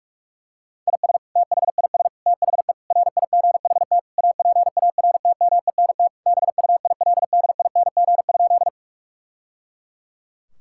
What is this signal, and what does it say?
Morse code, heard as a keyed tone, about 46 wpm, 700 Hz. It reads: IS THIS THE RIGHT APARTMENT BUILDING?
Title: Sentences from Top 1000 Words - Review Part 6 - 45wpm